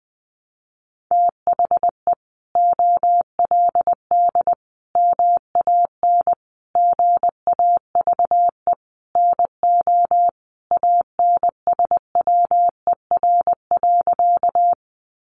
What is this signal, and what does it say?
Morse code, heard as a keyed tone, about 20 wpm, 700 Hz. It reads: THE OLD MAN GAVE NO ANSWER.
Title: Sentences from Top 500 Words 20wpm